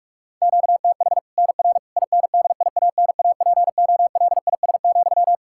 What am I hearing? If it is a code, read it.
Morse code, heard as a keyed tone, about 46 wpm, 700 Hz. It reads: QTH NR INDIANAPOLIS,
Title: Practice QSO - Exchange 1 - Beginning - 45wpm